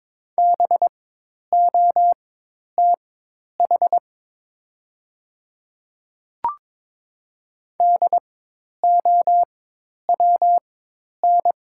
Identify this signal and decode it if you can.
Morse code, heard as a keyed tone, about 22 wpm, spaced as for 10 wpm, 700 Hz. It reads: BOTH DOWN
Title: 4x Character Spacing - Top 300 Words - 22wpm